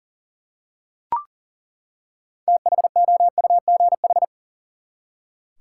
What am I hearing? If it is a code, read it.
Morse code, heard as a keyed tone, about 40 wpm, 700 Hz. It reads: THOUGH
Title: Six Letter Words - 40wpm